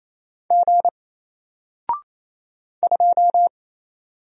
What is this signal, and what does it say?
Morse code, heard as a keyed tone, about 27 wpm, 700 Hz. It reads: G 2
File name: Single-Letter-Number-28wpm